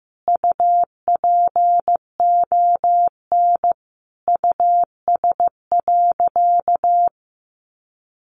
Morse code, heard as a keyed tone, about 15 wpm, 700 Hz. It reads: UPON US.
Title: Sentences from Top 1500 Words 15wpm